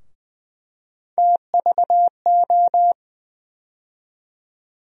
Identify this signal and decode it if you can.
Morse code, heard as a keyed tone, about 20 wpm, 700 Hz. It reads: TVO
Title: USA Call Sign Postfixes - 3 Letters 20wpm